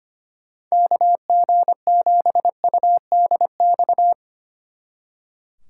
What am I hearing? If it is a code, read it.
Morse code, heard as a keyed tone, about 31 wpm, 700 Hz. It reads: KG7UDX
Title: Speed Racing - USA 2x3 Call Signs 25wpm